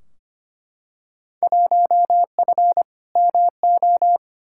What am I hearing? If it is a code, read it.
Morse code, heard as a keyed tone, about 25 wpm, 700 Hz. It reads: 1F MO
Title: Field Day Exchanges Encore - 25wpm